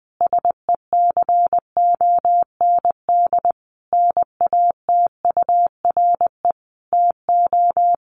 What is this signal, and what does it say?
Morse code, heard as a keyed tone, about 20 wpm, 700 Hz. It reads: SECOND NATURE TO
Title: Sentences from Top 600 Words 20wpm